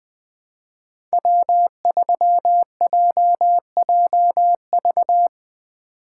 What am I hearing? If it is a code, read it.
Morse code, heard as a keyed tone, about 20 wpm, 700 Hz. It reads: W3JJV
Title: USA Call Signs - 5 Characters - 2x Character Spacing - 20wpm